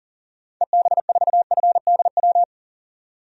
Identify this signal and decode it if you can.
Morse code, heard as a keyed tone, about 38 wpm, 700 Hz. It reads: EB4FDW
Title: DXCC-100 Call Signs - 6 Characters - 40wpm